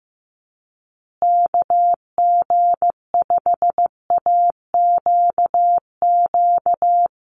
Morse code, heard as a keyed tone, about 15 wpm, 700 Hz. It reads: KG5AQQ
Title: USA Call Signs Repeated 3x 15wpm